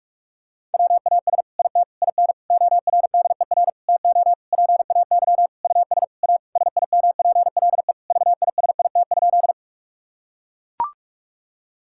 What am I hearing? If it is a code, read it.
Morse code, heard as a keyed tone, about 41 wpm, 690 Hz. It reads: WAS IT IN ORDER TO PAY US A SIMPLE VISIT?
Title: Sentences from Top 1000 Words - Review Part 6 - 45wpm